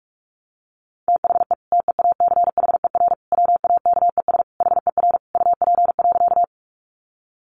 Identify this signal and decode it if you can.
Morse code, heard as a keyed tone, about 46 wpm, 700 Hz. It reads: THE TEACHER WAKES HER UP.
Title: Easy Phrases 45wpm